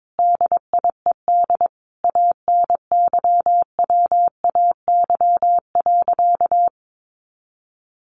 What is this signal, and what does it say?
Morse code, heard as a keyed tone, about 22 wpm, 700 Hz. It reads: DIED ANYWAY.
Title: A Sound Of Thunder 22wpm